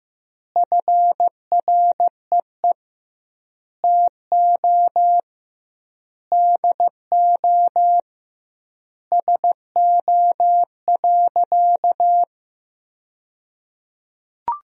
Morse code, heard as a keyed tone, about 15 wpm, 700 Hz. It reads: FREE TO DO SO.
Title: 2x Word-Spacing - Sentences from Top 500 Words - 15wpm